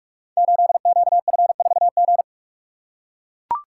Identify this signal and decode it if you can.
Morse code, heard as a keyed tone, about 44 wpm, 700 Hz. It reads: 8QF4G
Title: Code Groups - 5 Letter:Numbers - 45wpm